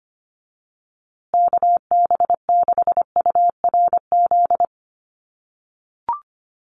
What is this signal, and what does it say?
Morse code, heard as a keyed tone, about 25 wpm, 700 Hz. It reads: KB6URZ
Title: USA Call Signs Repeated 3x 25wpm